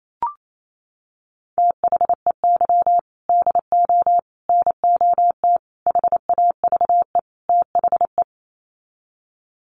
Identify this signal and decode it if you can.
Morse code, heard as a keyed tone, about 28 wpm, 700 Hz. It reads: THEY DO NOT HAVE THE
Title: Sets of 5 Words 28wpm